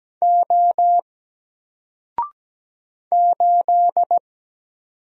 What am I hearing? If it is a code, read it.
Morse code, heard as a keyed tone, about 17 wpm, 700 Hz. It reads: O 8
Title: Single Character (Letter:Number:Punctuation:Prosign) 17wpm